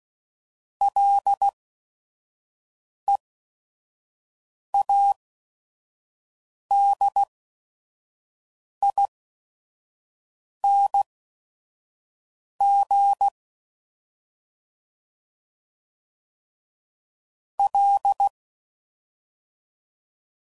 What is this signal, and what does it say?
Morse code, heard as a keyed tone, about 16 wpm, spaced as for 5 wpm, 800 Hz. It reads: LEADING L
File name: Prefixes / FLs & THs / Single Fs Ls